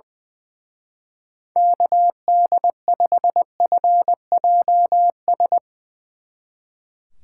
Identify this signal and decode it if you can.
Morse code, heard as a keyed tone, about 22 wpm, 700 Hz. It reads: KD5FJS
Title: Speed Racing - USA 2x3 Call Signs 20wpm